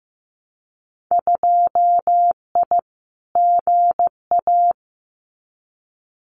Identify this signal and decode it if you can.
Morse code, heard as a keyed tone, about 15 wpm, 700 Hz. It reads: 2I GA
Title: Winter Field Day Exchanges 15wpm